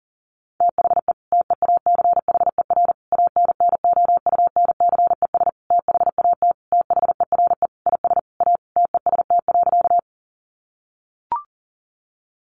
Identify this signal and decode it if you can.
Morse code, heard as a keyed tone, about 40 wpm, 700 Hz. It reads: THE TEACHER ANNOUNCES THAT THERE IS A TEST.
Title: Additional Easy Phrases 40wpm